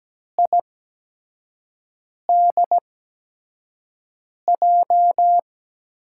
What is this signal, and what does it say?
Morse code, heard as a keyed tone, about 17 wpm, spaced as for 5 wpm, 700 Hz. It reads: IDJ
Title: USA Call Signs - 5 Characters - 8x Character Spacing - 17wpm